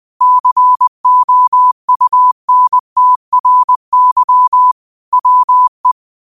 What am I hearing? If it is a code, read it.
Morse code, heard as a keyed tone, about 20 wpm, 1000 Hz. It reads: COUNTRY WE